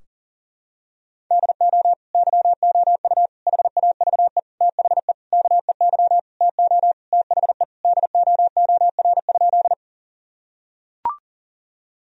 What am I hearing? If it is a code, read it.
Morse code, heard as a keyed tone, about 40 wpm, 700 Hz. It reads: DO YOU HAVE THE KEY TO THE DOOR?
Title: Sentences from Top 1000 Words - Review Part 4 - 40wpm